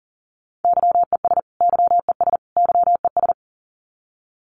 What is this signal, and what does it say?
Morse code, heard as a keyed tone, about 40 wpm, 700 Hz. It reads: YES YES YES
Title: Top 500 Words Repeated 40wpm